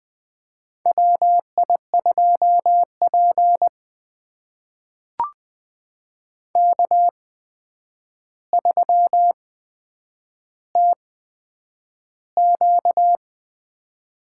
Morse code, heard as a keyed tone, about 20 wpm, spaced as for 9 wpm, 700 Hz. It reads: WI2P K3TQ
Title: USA Call Signs - 4 Characters - 8x Character Spacing - 20wpm